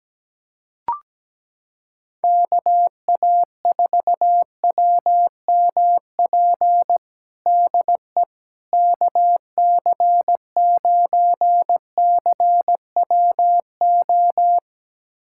Morse code, heard as a keyed tone, about 17 wpm, 700 Hz. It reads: KA4WMP DE KC9CWO
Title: USA Callsign DE Callsign - 6 Characters - 17wpm